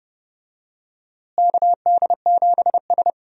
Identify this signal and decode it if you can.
Morse code, heard as a keyed tone, about 31 wpm, 700 Hz. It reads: KD7S
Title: USA Call Signs - 4 Characters - Repeated 3x - 30wpm